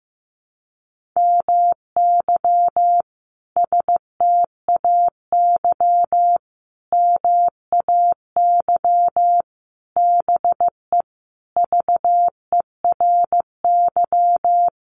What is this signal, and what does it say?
Morse code, heard as a keyed tone, about 15 wpm, 700 Hz. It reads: MY STAY MAY BE VERY
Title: Sentences from Top 1000 Words 15wpm